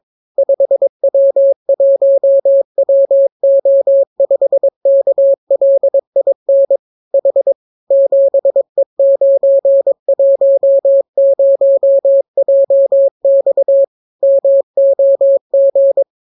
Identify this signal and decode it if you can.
Morse code, heard as a keyed tone, about 22 wpm, 550 Hz. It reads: 5W1WO5KLIN H 7E910JX MOG